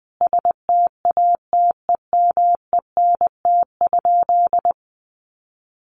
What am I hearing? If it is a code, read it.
Morse code, heard as a keyed tone, about 20 wpm, 700 Hz. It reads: STATEMENT?
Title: Sentences from Top 1200 Words 20wpm